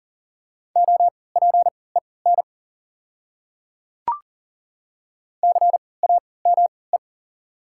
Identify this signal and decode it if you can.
Morse code, heard as a keyed tone, about 40 wpm, spaced as for 23 wpm, 690 Hz. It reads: OPEN CAME
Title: 3x Character Spacing - Top 300 Words - 40wpm